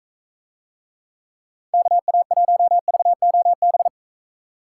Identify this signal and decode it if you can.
Morse code, heard as a keyed tone, about 45 wpm, 700 Hz. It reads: KA1VOB